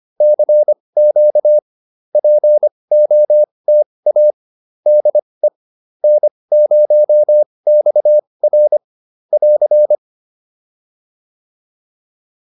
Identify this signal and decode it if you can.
Morse code, heard as a keyed tone, about 25 wpm, spaced as for 20 wpm, 600 Hz. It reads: CQ POTA DE N0XR <AR>